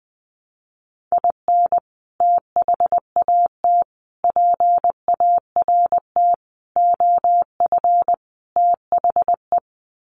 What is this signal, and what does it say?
Morse code, heard as a keyed tone, about 19 wpm, 700 Hz. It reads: IN THAT PART OF THE
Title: Sets of 5 Words Encore2 20wpm